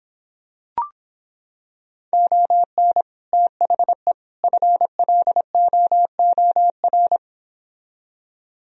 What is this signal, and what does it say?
Morse code, heard as a keyed tone, about 26 wpm, 700 Hz. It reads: ON THE FLOOR